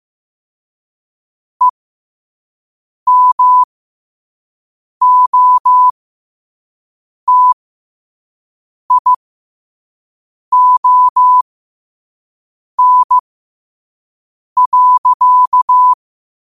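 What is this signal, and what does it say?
Morse code, heard as a keyed tone, about 15 wpm, spaced as for 5 wpm, 1000 Hz. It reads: EMOTION.
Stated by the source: Morse code practice at 5 wpm.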